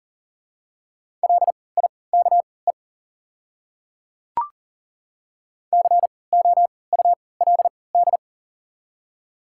Morse code, heard as a keyed tone, about 38 wpm, spaced as for 22 wpm, 700 Hz. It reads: LIKE COULD
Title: 3x Character Spacing - Top 200 Words - 40wpm